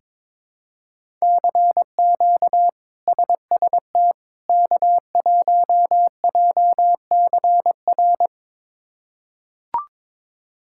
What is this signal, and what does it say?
Morse code, heard as a keyed tone, about 22 wpm, 700 Hz. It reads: CQ SST K1JCR
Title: K1USN SST QSO Practice - 22wpm